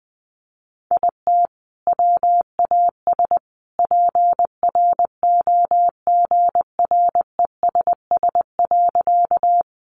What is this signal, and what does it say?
Morse code, heard as a keyed tone, about 20 wpm, 700 Hz. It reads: IT WAS PROGRESS.
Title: Sentences from Top 1400 Words 20wpm